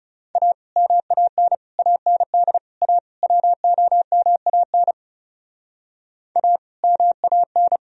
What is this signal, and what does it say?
Morse code, heard as a keyed tone, about 40 wpm, 700 Hz. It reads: A MAN AND A WOMAN A MAN